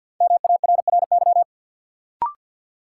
Morse code, heard as a keyed tone, about 54 wpm, 700 Hz. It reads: MARRY
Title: Sentences from Top 1200 Words - New Word Drill Alt - 50wpm